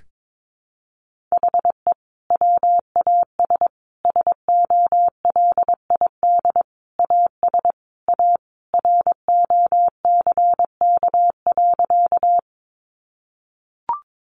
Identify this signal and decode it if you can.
Morse code, heard as a keyed tone, about 22 wpm, 700 Hz. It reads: HE WAS SOLID AS A ROCK.
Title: Sentences from Top 1500 Words 22wpm